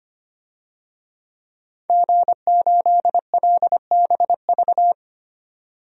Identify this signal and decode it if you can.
Morse code, heard as a keyed tone, about 25 wpm, 700 Hz. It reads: G8LBV
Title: DXCC-100 Call Signs - 5 Characters - 25wpm